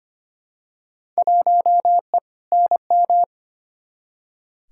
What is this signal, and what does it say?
Morse code, heard as a keyed tone, about 25 wpm, 700 Hz. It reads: 1E NM
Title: Field Day Exchanges - 25wpm